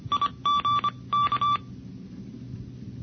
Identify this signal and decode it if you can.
Morse code, heard as a keyed tone, about 25 wpm, 1160 Hz. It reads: IG K
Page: NCCC SS Tape #2 - 1978 by: W1ARR, N6IG, N6NE and WA6HJV